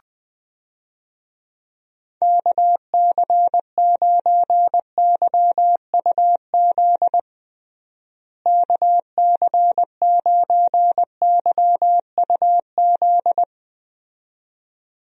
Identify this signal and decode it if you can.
Morse code, heard as a keyed tone, about 21 wpm, 700 Hz. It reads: KC9YUZ KC9YUZ
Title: USA Call Signs - 6 Characters - Repeated 2x - 20wpm